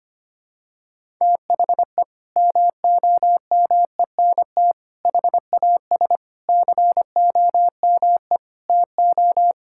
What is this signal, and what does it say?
Morse code, heard as a keyed tone, about 25 wpm, 700 Hz. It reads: THE MOMENT HAS COME TO
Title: Sentences from Top 1000 Words - Review Part 5 - 25wpm